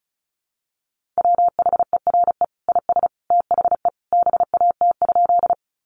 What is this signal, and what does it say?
Morse code, heard as a keyed tone, about 34 wpm, 700 Hz. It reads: WHERE IS THE BAT?
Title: Sentences from Top 1500 Words 35wpm